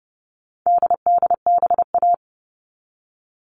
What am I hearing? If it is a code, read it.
Morse code, heard as a keyed tone, about 30 wpm, 700 Hz. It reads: DDBA
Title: All Cut Numbers - 4-Digit 30wpm